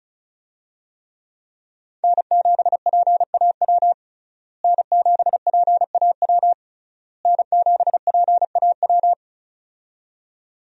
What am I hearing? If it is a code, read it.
Morse code, heard as a keyed tone, about 36 wpm, 700 Hz. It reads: N7PAW N7PAW N7PAW
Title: USA Call Signs - 5 Characters - Repeated 3x - 35wpm